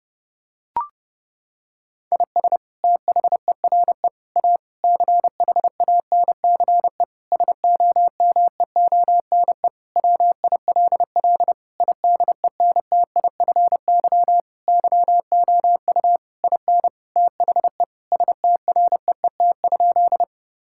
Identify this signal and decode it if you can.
Morse code, heard as a keyed tone, about 30 wpm, 700 Hz. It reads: IS THERE A CHANCE SOMEONE WILL IDENTIFY YOU IN THE STREET?
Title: Sentences from Top 1000 Words - Review Part 6 - 30wpm